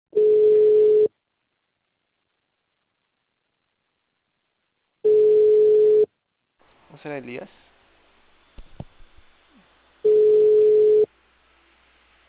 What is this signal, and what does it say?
Ringback tone (Europe, 425 Hz), with speech.